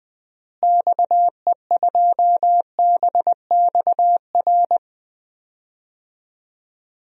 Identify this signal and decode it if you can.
Morse code, heard as a keyed tone, about 21 wpm, 700 Hz. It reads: XE2BXR
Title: DXCC-100 Call Signs - 6 Characters - Repeated 3x - 20wpm